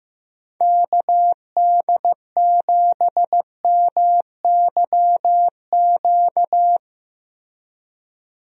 Morse code, heard as a keyed tone, about 15 wpm, 700 Hz. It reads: KD7MYQ
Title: USA Call Signs - 6 Characters - Repeated 3x - 15wpm